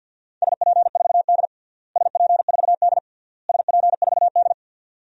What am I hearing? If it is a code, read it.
Morse code, heard as a keyed tone, about 49 wpm, 700 Hz. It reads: SP4D SP4D SP4D
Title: DXCC-100 Call Signs - 4 Characters - Repeated 3x - 50wpm